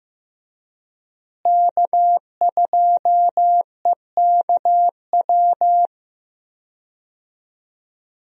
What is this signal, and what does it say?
Morse code, heard as a keyed tone, about 15 wpm, 700 Hz. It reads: K2EKW
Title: USA Call Signs - 5 Characters - Repeated 3x - 15wpm